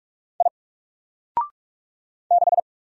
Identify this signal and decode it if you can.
Morse code, heard as a keyed tone, about 44 wpm, 700 Hz. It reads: I 6
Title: Single Character (Letter:Number:Punctuation:Prosign) 45wpm